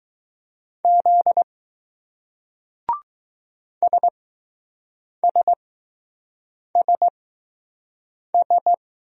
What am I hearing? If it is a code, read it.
Morse code, heard as a keyed tone, about 21 wpm, 700 Hz. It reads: Z S S S S